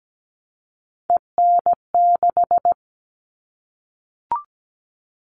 Morse code, heard as a keyed tone, about 17 wpm, 700 Hz. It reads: EN6
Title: All Cut Numbers - 3-Digit 17wpm